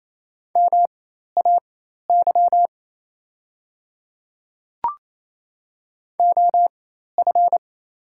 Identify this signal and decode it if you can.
Morse code, heard as a keyed tone, about 28 wpm, spaced as for 13 wpm, 700 Hz. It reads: MAY OF
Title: 4x Character Spacing - Top 100 Words - 28wpm